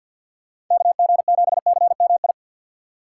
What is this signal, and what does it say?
Morse code, heard as a keyed tone, about 46 wpm, 700 Hz. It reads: KG7CMI
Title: USA Call Signs - 6 Characters - 6x Character Spacing - 50wpm